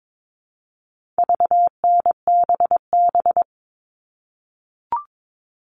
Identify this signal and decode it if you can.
Morse code, heard as a keyed tone, about 22 wpm, 700 Hz. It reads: VNBB
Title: All Cut Numbers - 4-Digit 22wpm